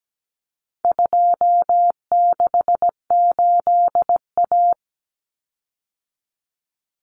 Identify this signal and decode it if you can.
Morse code, heard as a keyed tone, about 17 wpm, 700 Hz. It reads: 268A
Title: Common Cut Numbers - 4-Digit 17wpm